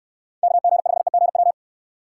Morse code, heard as a keyed tone, about 69 wpm, 700 Hz. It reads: KG5EGW